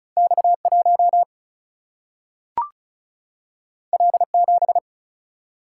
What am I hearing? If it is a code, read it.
Morse code, heard as a keyed tone, about 36 wpm, 700 Hz. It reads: X 1 L 7
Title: Single-Letter Single-Number 35wpm